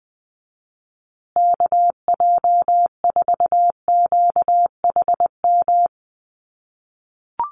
Morse code, heard as a keyed tone, about 20 wpm, 700 Hz. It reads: KJ4QHM
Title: USA Call Signs Repeated 2x 20wpm